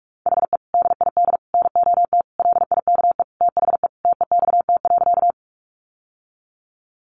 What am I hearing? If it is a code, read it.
Morse code, heard as a keyed tone, about 44 wpm, 700 Hz. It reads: HE DID NOT LIKE THE TEXT.
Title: Sentences from Top 1200 Words 45wpm